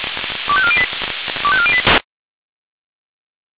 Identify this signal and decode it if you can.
Five-tone selective call (ZVEI), heard as quick runs of five tones, tones of about 70 ms. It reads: 25408 25408